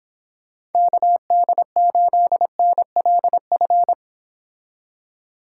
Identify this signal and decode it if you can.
Morse code, heard as a keyed tone, about 29 wpm, 700 Hz. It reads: KD8NLF